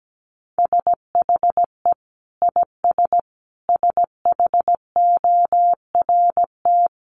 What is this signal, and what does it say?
Morse code, heard as a keyed tone, about 17 wpm, 700 Hz. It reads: SHE IS SHORT
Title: Sentences from Top 1200 Words 17wpm